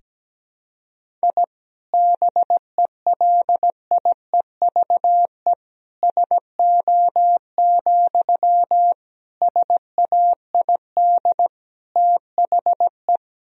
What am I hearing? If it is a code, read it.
Morse code, heard as a keyed tone, about 17 wpm, 700 Hz. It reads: I BELIEVE SO, SAID THE
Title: Sentences from Top 1000 Words - Review Part 4 - 17wpm